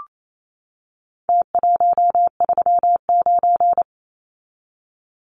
Morse code, heard as a keyed tone, about 28 wpm, 700 Hz. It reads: T139
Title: Numbers - Contest Style - 4-Digit 28wpm